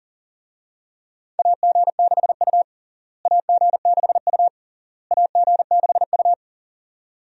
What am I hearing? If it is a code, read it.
Morse code, heard as a keyed tone, about 40 wpm, 700 Hz. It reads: AG6U AG6U AG6U
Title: USA Call Signs - 4 Characters - Repeated 3x - 40wpm